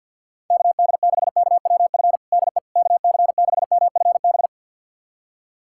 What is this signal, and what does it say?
Morse code, heard as a keyed tone, about 47 wpm, 700 Hz. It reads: KD6KWF DE KC6MFB
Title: USA Callsign DE Callsign - 6 Characters - 50wpm